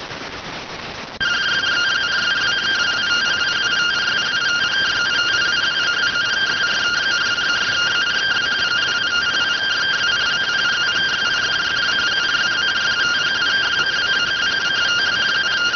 Two-tone teleprinter signal, about 170 Hz shift.